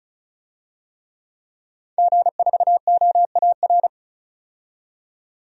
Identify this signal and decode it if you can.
Morse code, heard as a keyed tone, about 34 wpm, 700 Hz. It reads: G4OAR
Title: DXCC-100 Call Signs - 5 Characters - 35wpm